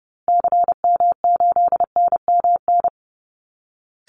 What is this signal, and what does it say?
Morse code, heard as a keyed tone, about 30 wpm, 700 Hz. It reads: CM8NMN
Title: Call Signs Repeated 3x - 30wpm